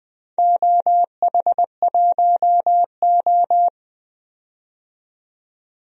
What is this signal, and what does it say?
Morse code, heard as a keyed tone, about 20 wpm, 700 Hz. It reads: OH1O
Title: DXCC-100 Call Signs - 4 Characters - 20wpm